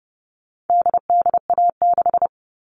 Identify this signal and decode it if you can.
Morse code, heard as a keyed tone, about 30 wpm, 700 Hz. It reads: DDA6
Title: All Cut Numbers - 4-Digit 30wpm